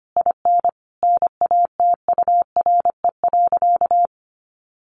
Morse code, heard as a keyed tone, about 25 wpm, 700 Hz. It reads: IN NATURE.
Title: Sentences from Top 1400 Words 25wpm